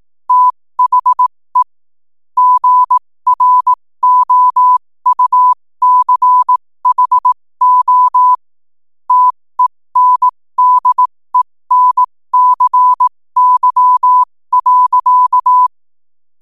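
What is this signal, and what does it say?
Morse code, heard as a keyed tone, about 18 wpm, spaced as for 15 wpm, 1000 Hz. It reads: THE GROUCHO TENDENCY.
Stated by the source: 15 wpm Morse code from RNZN Communicators Association.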